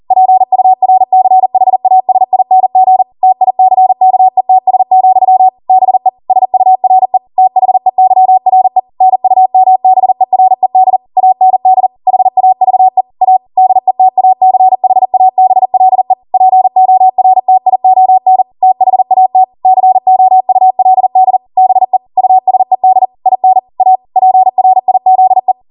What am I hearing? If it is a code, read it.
Morse code, heard as a keyed tone, about 40 wpm, 750 Hz. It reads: PURCHASING TICKETS, BE SURE THEYRE NUMBERED AND HAVE A DETACHABLE PORTION THAT COULD BE USED IN A PRIZE